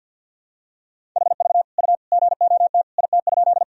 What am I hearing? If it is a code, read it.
Morse code, heard as a keyed tone, about 48 wpm, 700 Hz. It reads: HV U GOT IT?
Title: Sentences from Top 300 Words and Abbreviations 50wpm